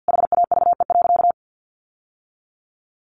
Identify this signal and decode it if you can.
Morse code, heard as a keyed tone, about 50 wpm, 700 Hz. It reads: HAVE.
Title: Sentences from Top 1100 Words 50wpm